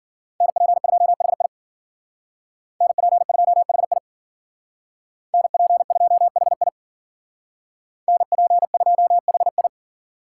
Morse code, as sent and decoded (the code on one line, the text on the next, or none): -. .--. ..--- .... .. / -. .--. ..--- .... .. / -. .--. ..--- .... .. / -. .--. ..--- .... ..
NP2HI NP2HI NP2HI NP2HI